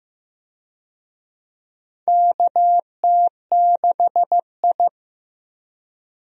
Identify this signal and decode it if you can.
Morse code, heard as a keyed tone, about 15 wpm, 700 Hz. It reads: KT6I
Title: USA Call Signs - 4 Characters - Repeated 3x - 15wpm